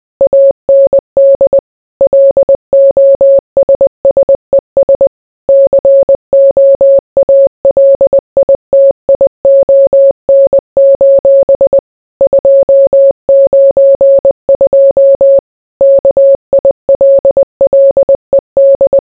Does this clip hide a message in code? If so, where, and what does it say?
Yes: Morse code from 0.2 to 19.0 s, heard as a keyed tone, about 20 wpm, 550 Hz. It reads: AND LOSSES COALITION: 292 KILLED